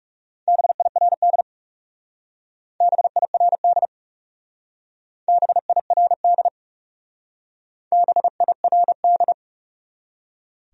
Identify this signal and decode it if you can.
Morse code, heard as a keyed tone, about 36 wpm, 700 Hz. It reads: BIRD BIRD BIRD BIRD
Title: Speed-Racing Top 500 Words 30wpm